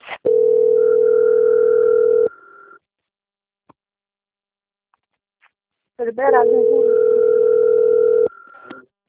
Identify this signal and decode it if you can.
Ringback tone (North America), 440 + 480 Hz, with speech.
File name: out- - - - - .20493